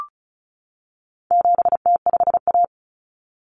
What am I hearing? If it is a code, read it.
Morse code, heard as a keyed tone, about 34 wpm, 700 Hz. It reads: 7T5A
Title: Common Cut Numbers - 4-Digit 35wpm